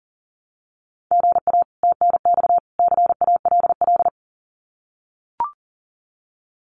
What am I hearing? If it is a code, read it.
Morse code, heard as a keyed tone, about 40 wpm, 700 Hz. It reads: GA TNX CALL
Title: Common QSO Phrases 40wpm